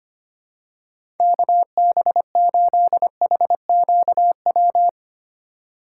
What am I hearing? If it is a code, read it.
Morse code, heard as a keyed tone, about 25 wpm, 700 Hz. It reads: KB8HQW
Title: USA Call Signs - 6 Characters - 3x Character Spacing - 25wpm